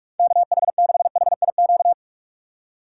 Morse code, heard as a keyed tone, about 46 wpm, 700 Hz. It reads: KH6HIQ
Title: USA Call Signs - 4x Character Spacing - 45wpm